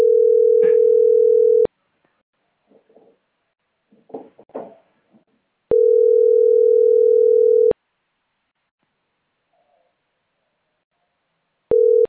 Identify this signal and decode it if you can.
Ringback tone (North America), 440 + 480 Hz.